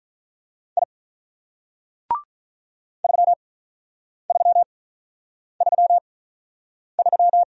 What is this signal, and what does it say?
Morse code, heard as a keyed tone, about 47 wpm, 700 Hz. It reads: I 3 3 3 3